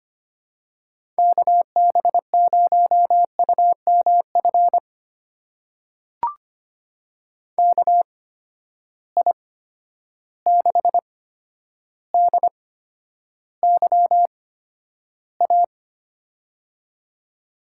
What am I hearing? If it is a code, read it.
Morse code, heard as a keyed tone, about 25 wpm, spaced as for 11 wpm, 700 Hz. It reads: KB0UMF KI6DYA
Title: USA Call Signs - 6 Characters - 8x Character Spacing - 25wpm